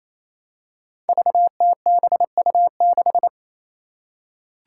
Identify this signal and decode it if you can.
Morse code, heard as a keyed tone, about 27 wpm, 700 Hz. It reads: VTBU6
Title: Five Challenging Letter-Number Combinations 28wpm